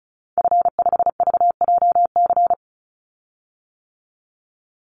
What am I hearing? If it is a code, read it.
Morse code, heard as a keyed tone, about 34 wpm, 700 Hz. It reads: F5VJC
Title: Call Signs - 35wpm